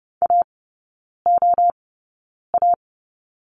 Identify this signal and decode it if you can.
Morse code, heard as a keyed tone, about 30 wpm, spaced as for 9 wpm, 700 Hz. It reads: AOA
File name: Lesson 18e - Letter m - Three Characters 30wpm